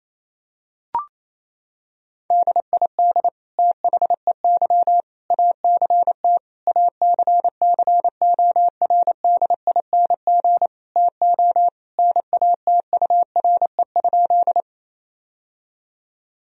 Morse code, heard as a keyed tone, about 27 wpm, 700 Hz. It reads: DID THEY ACT ACCORDING TO NATURE?
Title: Sentences from Top 1000 Words - Review Part 4 - 28wpm